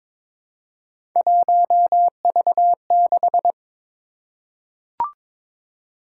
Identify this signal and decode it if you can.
Morse code, heard as a keyed tone, about 22 wpm, 700 Hz. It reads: 1V6
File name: practice / 22wpm / Three Letter-Number Combinations - 22wpm